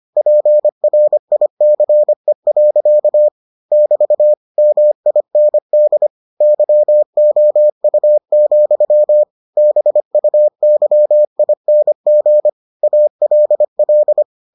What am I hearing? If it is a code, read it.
Morse code, heard as a keyed tone, about 25 wpm, 600 Hz. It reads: PRICE. <BT> MIND YOU, BUYING ALL